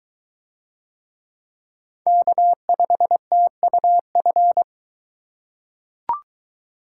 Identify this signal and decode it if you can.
Morse code, heard as a keyed tone, about 20 wpm, 700 Hz. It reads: K5TUF